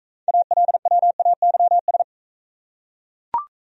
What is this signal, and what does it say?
Morse code, heard as a keyed tone, about 40 wpm, 700 Hz. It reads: ALWAYS